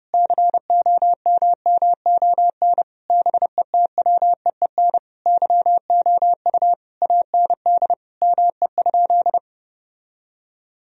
Morse code, heard as a keyed tone, about 30 wpm, 700 Hz. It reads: COMMON BETWEEN YOU AND ME?
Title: Sentences from Top 1000 Words - Review Part 3 - 30wpm